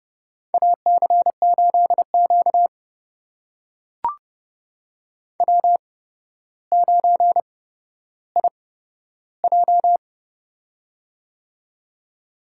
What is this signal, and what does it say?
Morse code, heard as a keyed tone, about 30 wpm, spaced as for 13 wpm, 700 Hz. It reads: AC8Q W9IJ
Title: USA Call Signs - 4 Characters - 8x Character Spacing - 30wpm